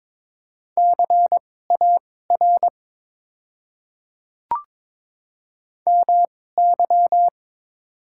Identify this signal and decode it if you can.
Morse code, heard as a keyed tone, about 22 wpm, spaced as for 16 wpm, 700 Hz. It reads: CAR MY
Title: 2x Character Spacing - Top 300 Words - 22wpm